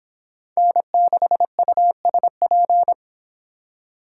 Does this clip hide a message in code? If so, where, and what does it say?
Yes: Morse code from 0.6 to 2.9 s, heard as a keyed tone, about 29 wpm, 700 Hz. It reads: N6USP